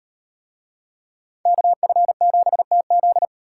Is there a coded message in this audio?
Yes: Morse code from 1.5 to 3.2 s, heard as a keyed tone, about 38 wpm, 700 Hz. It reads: KF7TZ